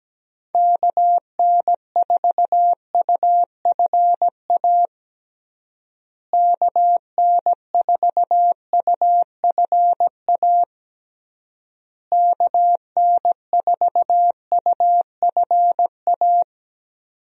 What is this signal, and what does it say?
Morse code, heard as a keyed tone, about 17 wpm, 700 Hz. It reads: KN4UFA KN4UFA KN4UFA
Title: USA Call Signs - 6 Characters - Repeated 3x - 17wpm